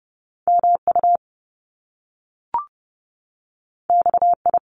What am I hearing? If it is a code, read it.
Morse code, heard as a keyed tone, about 30 wpm, 700 Hz. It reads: MU XI
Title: Top Two Letter Combinations in Words 30wpm